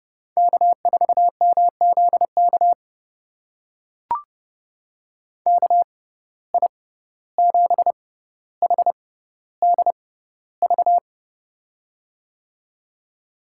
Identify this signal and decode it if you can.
Morse code, heard as a keyed tone, about 32 wpm, spaced as for 15 wpm, 700 Hz. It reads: K4MZK KI7HDV
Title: USA Call Signs - 6x Character Spacing - 30wpm